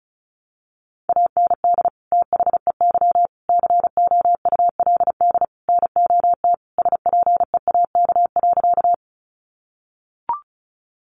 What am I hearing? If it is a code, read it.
Morse code, heard as a keyed tone, about 34 wpm, 700 Hz. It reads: AND THEY COULD NOT SPEAK.
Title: Sentences from Top 1100 Words 35wpm